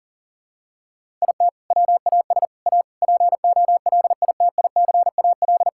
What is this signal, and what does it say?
Morse code, heard as a keyed tone, about 37 wpm, 700 Hz. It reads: IT WAS A POLITICAL
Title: Sentences from Top 1000 Words - Review Part 5 - 40wpm